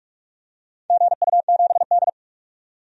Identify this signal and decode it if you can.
Morse code, heard as a keyed tone, about 43 wpm, 700 Hz. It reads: GU7D
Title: DXCC-100 Call Signs - 4 Characters - 45wpm